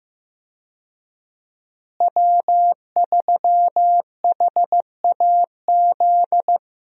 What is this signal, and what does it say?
Morse code, heard as a keyed tone, about 15 wpm, 700 Hz. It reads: W3HAZ
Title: USA Call Signs - 5 Characters - Repeated 3x - 15wpm